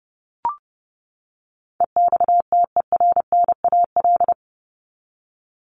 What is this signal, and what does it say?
Morse code, heard as a keyed tone, about 30 wpm, 700 Hz. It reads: EXTERNAL
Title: Top 1751-2000 Words 30wpm